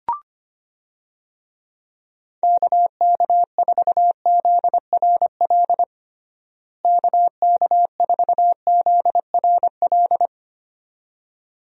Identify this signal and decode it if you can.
Morse code, heard as a keyed tone, about 25 wpm, 700 Hz. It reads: KK4ZRL KK4ZRL
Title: USA Call Signs - 6 Characters - Repeated 2x - 25wpm